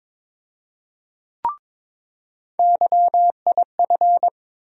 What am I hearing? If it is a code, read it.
Morse code, heard as a keyed tone, about 22 wpm, 700 Hz. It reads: YIF
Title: USA Call Sign Postfixes - 3 Letters 22wpm